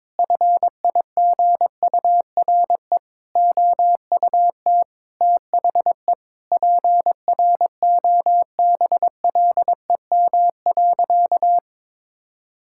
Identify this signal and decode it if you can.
Morse code, heard as a keyed tone, about 22 wpm, 700 Hz. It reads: FIGURE OUT THE PROBLEM.
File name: practice / 22wpm / Sentences with 4-Word Phrases - 22wpm